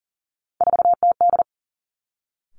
Morse code, heard as a keyed tone, about 40 wpm, 700 Hz. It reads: 4TD
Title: Three Challenging Letter-Number Combinations 40wpm